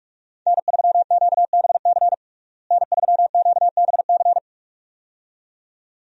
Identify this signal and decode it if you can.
Morse code, heard as a keyed tone, about 46 wpm, 700 Hz. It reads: N3QBC N3QBC
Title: USA Call Signs - 5 Characters - Repeated 2x - 45wpm